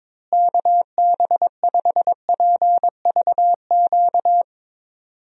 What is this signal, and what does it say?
Morse code, heard as a keyed tone, about 22 wpm, 700 Hz. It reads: KB5PVQ
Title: USA Call Signs - 6 Characters - Repeated 3x - 22wpm